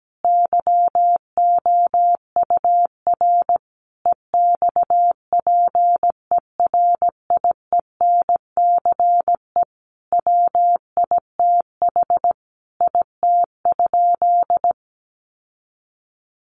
Morse code, heard as a keyed tone, about 17 wpm, 700 Hz. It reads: YOUR EXPERIENCE WITH IT?
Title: Sentences from Top 600 Words 17wpm